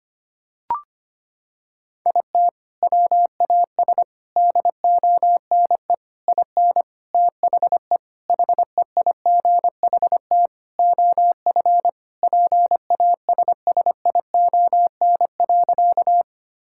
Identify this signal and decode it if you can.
Morse code, heard as a keyed tone, about 25 wpm, 700 Hz. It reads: IT WAS DONE IN THE HEIGHT OF PASSION.
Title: Sentences from Top 1600 Words 25wpm